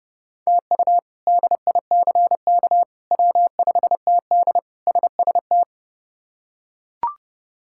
Morse code, heard as a keyed tone, about 30 wpm, 700 Hz. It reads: TU DICK W5TD SST
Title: K1USN SST QSO Practice - 30wpm